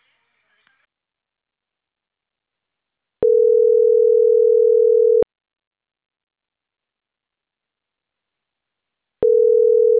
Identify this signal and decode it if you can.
Ringback tone (North America), 440 + 480 Hz.
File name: CENTRAL CAEA / out agcaea03